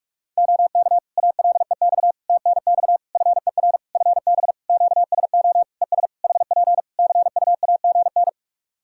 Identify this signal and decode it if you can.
Morse code, heard as a keyed tone, about 44 wpm, 700 Hz. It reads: OK ALEX TNX FER FB QSO ES HP CUAGN
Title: Practice QSO - Exchange 4 - Ending - 45wpm